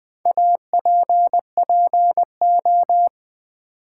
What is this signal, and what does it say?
Morse code, heard as a keyed tone, about 20 wpm, 700 Hz. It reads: APPO
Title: Build A Word - 4 Letters At A Time - 11-Letter Words - 20wpm